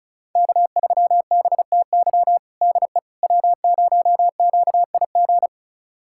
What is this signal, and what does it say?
Morse code, heard as a keyed tone, about 34 wpm, 700 Hz. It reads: K3BTY DE W0QIG
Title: USA Callsign DE Callsign - 5 Characters - 35wpm